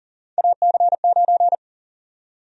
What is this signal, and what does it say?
Morse code, heard as a keyed tone, about 40 wpm, 700 Hz. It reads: AC9
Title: USA Call Sign Prefixes - 40wpm